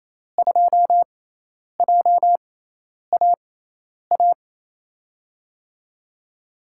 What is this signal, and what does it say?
Morse code, heard as a keyed tone, about 28 wpm, spaced as for 10 wpm, 700 Hz. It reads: 2JAA
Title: USA Call Signs - 5 Characters - 6x Character Spacing - 28wpm